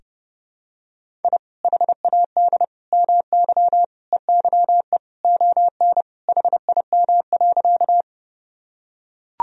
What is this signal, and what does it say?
Morse code, heard as a keyed tone, about 30 wpm, 700 Hz. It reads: I HAD MY EYE ON HIM.
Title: Sentences from Top 1000 Words - Review Part 1 - 30wpm